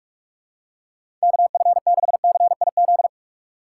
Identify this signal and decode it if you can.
Morse code, heard as a keyed tone, about 46 wpm, 700 Hz. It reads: KF6CIZ
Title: USA Call Signs - 6 Characters - 2x Character Spacing - 45wpm